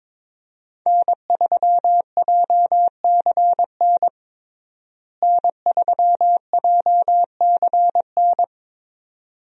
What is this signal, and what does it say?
Morse code, heard as a keyed tone, about 22 wpm, 700 Hz. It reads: N3JCN N3JCN
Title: USA Call Signs - 5 Characters - Repeated 2x - 22wpm